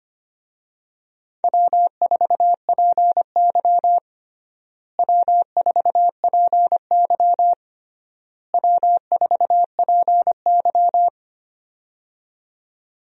Morse code, heard as a keyed tone, about 25 wpm, 700 Hz. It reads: W4PY W4PY W4PY
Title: USA Call Signs - 4 Characters - Repeated 3x - 25wpm